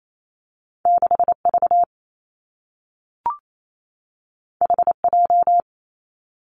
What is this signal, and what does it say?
Morse code, heard as a keyed tone, about 28 wpm, 700 Hz. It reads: 6 V H J
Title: Two Challenging Letter-Number Combinations 28wpm